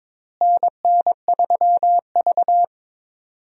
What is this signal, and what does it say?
Morse code, heard as a keyed tone, about 22 wpm, 700 Hz. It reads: NN3V
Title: USA Call Signs - 4 Characters - Repeated 2x - 22wpm